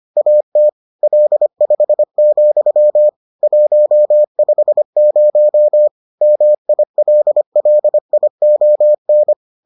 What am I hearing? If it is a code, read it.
Morse code, heard as a keyed tone, about 25 wpm, 600 Hz. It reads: AT L5, 150 MILLION